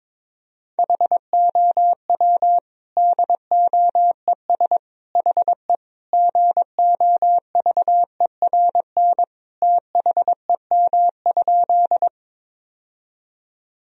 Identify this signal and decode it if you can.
Morse code, heard as a keyed tone, about 22 wpm, 700 Hz. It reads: HOW DOES HE GOVERN THEM?
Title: Sentences from Top 1000 Words - Review Part 2 - 22wpm